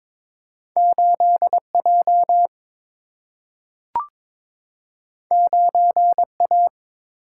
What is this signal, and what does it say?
Morse code, heard as a keyed tone, about 22 wpm, 700 Hz. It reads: 8J 9A
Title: Single-Letter Single-Number 22wpm